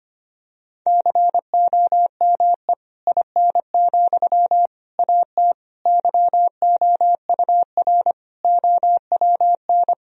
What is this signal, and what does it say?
Morse code, heard as a keyed tone, about 25 wpm, 700 Hz. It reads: COME IN, AT YOUR OWN
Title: Sentences from Top 1000 Words - Review Part 4 - 25wpm